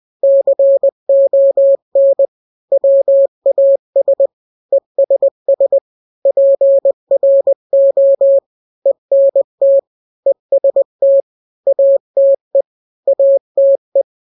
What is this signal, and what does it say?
Morse code, heard as a keyed tone, about 20 wpm, 550 Hz. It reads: CON WAS ESS PRO ENT EST ATE ATE
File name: n-grams / 3-grams 20 18 WPM